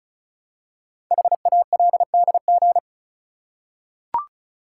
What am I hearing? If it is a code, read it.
Morse code, heard as a keyed tone, about 36 wpm, 700 Hz. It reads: HALDG
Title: Code Groups - 5 Letters - 35wpm